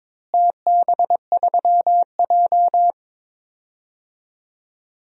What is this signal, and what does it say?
Morse code, heard as a keyed tone, about 22 wpm, 700 Hz. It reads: TB3J
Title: DXCC-100 Call Signs - 4 Characters - Repeated 2x - 22wpm